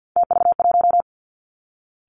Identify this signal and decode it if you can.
Morse code, heard as a keyed tone, about 50 wpm, 700 Hz. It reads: TV1
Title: Three Challenging Letter-Number Combinations 50wpm